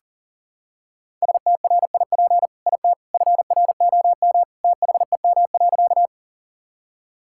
Morse code, heard as a keyed tone, about 40 wpm, 700 Hz. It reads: STRIP IT FROM THEM.
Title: Sentences from Top 1600 Words 40wpm